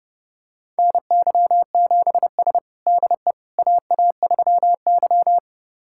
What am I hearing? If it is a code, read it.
Morse code, heard as a keyed tone, about 30 wpm, 700 Hz. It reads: NY7S DE AA3Y
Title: USA Callsign DE Callsign - 4 Characters - 30wpm